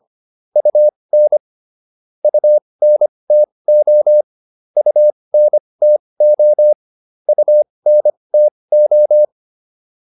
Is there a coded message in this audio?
Yes: Morse code from 0.6 to 9.2 s, heard as a keyed tone, about 25 wpm, spaced as for 20 wpm, 600 Hz. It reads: UN UNTO UNTO UNTO